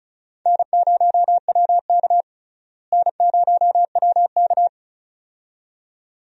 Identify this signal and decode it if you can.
Morse code, heard as a keyed tone, about 36 wpm, 700 Hz. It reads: N0WK N0WK
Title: USA Call Signs - 4 Characters - Repeated 2x - 35wpm